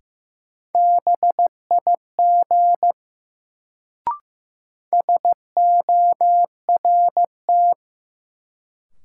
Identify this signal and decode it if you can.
Morse code, heard as a keyed tone, about 15 wpm, 700 Hz. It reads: BIG SORT
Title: Sets of 2 Words - New Word Drill - 15wpm